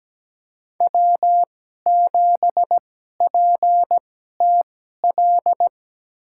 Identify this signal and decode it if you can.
Morse code, heard as a keyed tone, about 17 wpm, spaced as for 12 wpm, 700 Hz. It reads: W7PTL
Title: USA Call Signs - 5 Characters - 2x Character Spacing - 17wpm